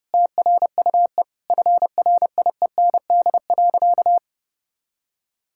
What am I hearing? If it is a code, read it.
Morse code, heard as a keyed tone, about 30 wpm, 700 Hz. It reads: TRUE FRIEND.
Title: Sentences from Top 1000 Words - Review Part 6 - 30wpm